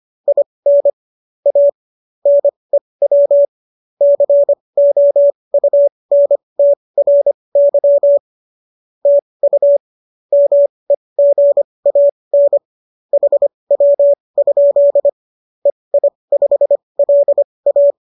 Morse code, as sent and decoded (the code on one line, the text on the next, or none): .. -. / .- / -. .-- / -.-. --- ..- -. - .-. -.-- / - ..- / -- . --. .- -. / .... .-- ..--.. / . .. ..... .-.. .-
IN A NEW COUNTRY TU MEGAN HW? EI5LA